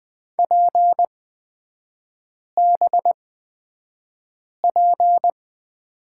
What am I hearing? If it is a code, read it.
Morse code, heard as a keyed tone, about 20 wpm, spaced as for 5 wpm, 700 Hz. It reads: PBP
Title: 250 Lesson 40e - BK - Three Characters 20wpm